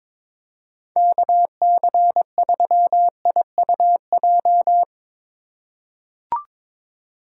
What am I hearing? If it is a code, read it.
Morse code, heard as a keyed tone, about 22 wpm, 700 Hz. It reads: KC3IUJ
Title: USA Call Signs - 6 Characters - 3x Character Spacing - 22wpm